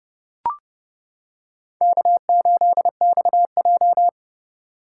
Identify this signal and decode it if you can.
Morse code, heard as a keyed tone, about 30 wpm, 700 Hz. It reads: K8XJ
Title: K1USN SST QSO Practice - 30wpm